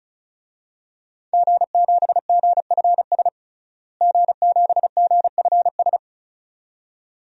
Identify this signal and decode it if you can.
Morse code, heard as a keyed tone, about 34 wpm, 700 Hz. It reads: G7GFS G7GFS
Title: DXCC-100 Call Signs - 5 Characters - Repeated 2x - 35wpm